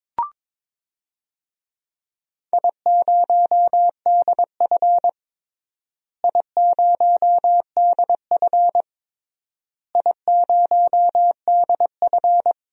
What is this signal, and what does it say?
Morse code, heard as a keyed tone, about 22 wpm, 700 Hz. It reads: I0DF I0DF I0DF
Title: DXCC-100 Call Signs - 4 Characters - Repeated 3x - 22wpm